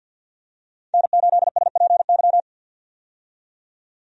Morse code, heard as a keyed tone, about 50 wpm, 700 Hz. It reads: N8SPY
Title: USA Call Signs - 5 Characters - 50wpm